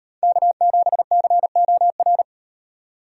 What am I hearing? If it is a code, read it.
Morse code, heard as a keyed tone, about 38 wpm, 700 Hz. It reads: K7COR